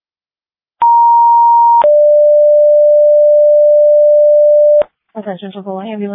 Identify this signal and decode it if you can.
Two-tone page, heard as a one-second tone, then a three-second tone, about 940 then 590 Hz, with speech. It is heard from 0.8 to 4.8 s.